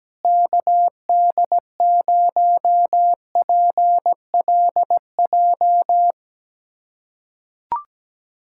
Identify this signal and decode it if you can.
Morse code, heard as a keyed tone, about 17 wpm, 700 Hz. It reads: KD0PLJ
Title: USA Callsign DE Callsign - 6 Characters - 17wpm